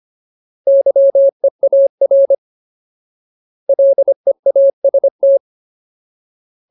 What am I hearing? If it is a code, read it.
Morse code, heard as a keyed tone, about 25 wpm, 550 Hz. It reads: YEAR LEAST
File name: Ultimate Word List / 25WPM Ultimate